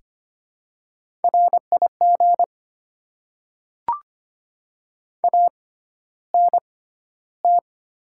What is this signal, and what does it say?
Morse code, heard as a keyed tone, about 25 wpm, spaced as for 13 wpm, 700 Hz. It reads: RIG ANT
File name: Top 20 QSO Elements - 6x Character Spacing - 25wpm